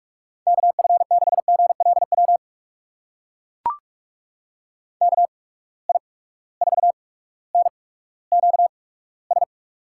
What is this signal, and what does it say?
Morse code, heard as a keyed tone, about 44 wpm, spaced as for 19 wpm, 700 Hz. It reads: KF6GLW KI4NQS
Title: USA Call Signs - 6 Characters - 8x Character Spacing - 45wpm